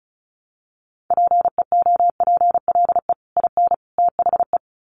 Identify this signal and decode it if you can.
Morse code, heard as a keyed tone, about 34 wpm, 700 Hz. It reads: PEOPLE IN THE
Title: Sets of 3 Words 35wpm